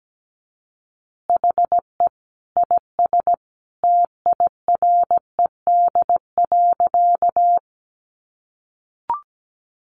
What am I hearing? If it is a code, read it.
Morse code, heard as a keyed tone, about 17 wpm, 700 Hz. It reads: HE IS TIRED.
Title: Additional Easy Phrases 17wpm